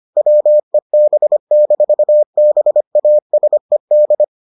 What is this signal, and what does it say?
Morse code, heard as a keyed tone, about 25 wpm, 600 Hz. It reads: WEB-BASED